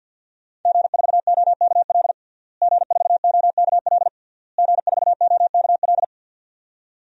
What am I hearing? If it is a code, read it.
Morse code, heard as a keyed tone, about 50 wpm, 700 Hz. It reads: G4OKL G4OKL G4OKL
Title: DXCC-100 Call Signs - 5 Characters - Repeated 3x - 50wpm